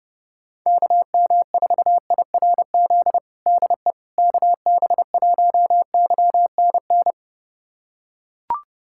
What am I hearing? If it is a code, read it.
Morse code, heard as a keyed tone, about 32 wpm, 700 Hz. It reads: KM4IRZ DE KB1YNN
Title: USA Callsign DE Callsign - 6 Characters - 30wpm